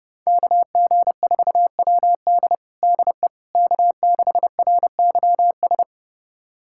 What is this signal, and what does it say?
Morse code, heard as a keyed tone, about 30 wpm, 700 Hz. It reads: KG4WD DE K6RYS
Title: USA Callsign DE Callsign - 5 Characters - 30wpm